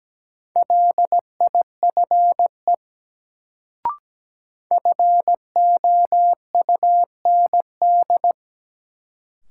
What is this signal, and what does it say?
Morse code, heard as a keyed tone, about 17 wpm, 700 Hz. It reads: LIFE FOUND
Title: Sentences from Top 300 Words - New Word Drill - 17wpm